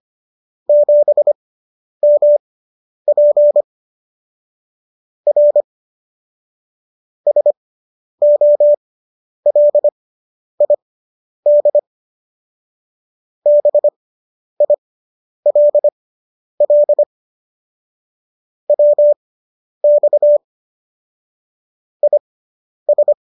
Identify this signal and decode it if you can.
Morse code, heard as a keyed tone, about 25 wpm, spaced as for 10 wpm, 590 Hz. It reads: 7MP R SOLID BILL WX IS